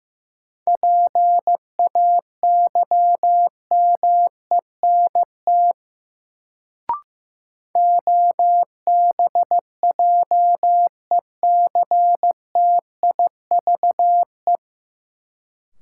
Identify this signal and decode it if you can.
Morse code, heard as a keyed tone, about 15 wpm, 700 Hz. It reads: PAYMENT OBJECTIVE
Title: Sentences from Top 1200 Words - New Word Drill - 15wpm